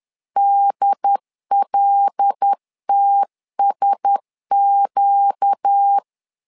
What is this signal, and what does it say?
Morse code, heard as a keyed tone, about 10 wpm, 800 Hz. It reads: DLTSQ
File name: LEZIONI / 5 LBVC / 5 L 10wpm 2259